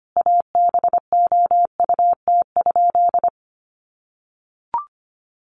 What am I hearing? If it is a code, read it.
Morse code, heard as a keyed tone, about 25 wpm, 700 Hz. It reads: ABOUT?
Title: Sentences from Top 100 Words 25wpm